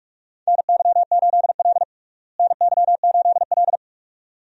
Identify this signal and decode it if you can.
Morse code, heard as a keyed tone, about 46 wpm, 700 Hz. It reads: NY8L NY8L
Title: USA Call Signs - 4 Characters - Repeated 2x - 45wpm